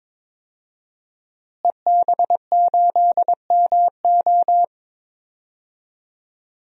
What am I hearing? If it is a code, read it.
Morse code, heard as a keyed tone, about 22 wpm, 700 Hz. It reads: EB8MO
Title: DXCC-100 Call Signs - 5 Characters - Repeated 2x - 22wpm